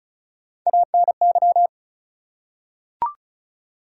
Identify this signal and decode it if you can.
Morse code, heard as a keyed tone, about 36 wpm, 700 Hz. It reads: ANY
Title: Three Letter Words - 35wpm